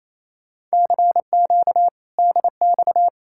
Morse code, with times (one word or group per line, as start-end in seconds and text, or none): 0.7-1.9 CQ
2.2-3.1 DX